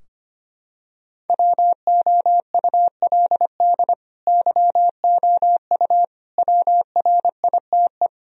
Morse code, heard as a keyed tone, about 25 wpm, 700 Hz. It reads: WOULD YOU WRITE
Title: Sentences from Top 1000 Words - Review Part 2 - 25wpm